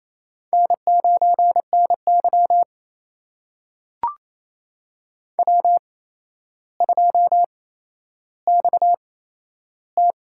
Morse code, heard as a keyed tone, about 28 wpm, spaced as for 12 wpm, 700 Hz. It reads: N9NY W2XT
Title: USA Call Signs - 4 Characters - 8x Character Spacing - 28wpm